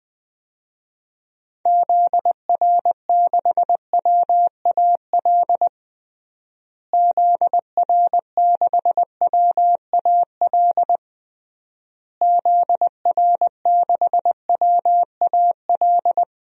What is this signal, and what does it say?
Morse code, heard as a keyed tone, about 21 wpm, 700 Hz. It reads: ZR6WAL ZR6WAL ZR6WAL
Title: DXCC-100 Call Signs - 6 Characters - Repeated 3x - 20wpm